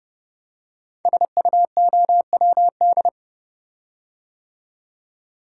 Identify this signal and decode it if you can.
Morse code, heard as a keyed tone, about 30 wpm, 700 Hz. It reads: SUOWD
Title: Code Groups - 5 Letters - 30wpm